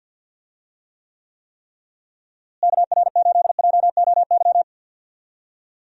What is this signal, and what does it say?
Morse code, heard as a keyed tone, about 52 wpm, 700 Hz. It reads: KR8JOY
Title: USA Call Signs - 6 Characters - Repeated 3x - 50wpm